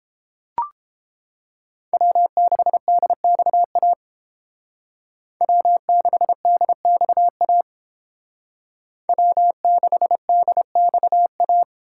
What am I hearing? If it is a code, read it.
Morse code, heard as a keyed tone, about 29 wpm, 700 Hz. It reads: W6DXA W6DXA W6DXA